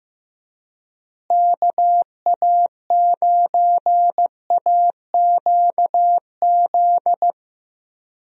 Morse code, heard as a keyed tone, about 15 wpm, 700 Hz. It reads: KA9AQZ
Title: USA Call Signs - 6 Characters - Repeated 3x - 15wpm